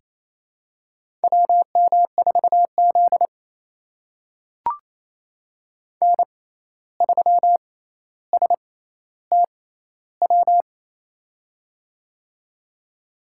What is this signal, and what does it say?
Morse code, heard as a keyed tone, about 28 wpm, spaced as for 13 wpm, 700 Hz. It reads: WM4Z N3STW
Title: USA Call Signs - 6x Character Spacing - 28wpm